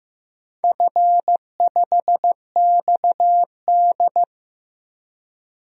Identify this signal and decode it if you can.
Morse code, heard as a keyed tone, about 15 wpm, 700 Hz. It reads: F5XD
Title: DXCC-100 Call Signs - 4 Characters - 15wpm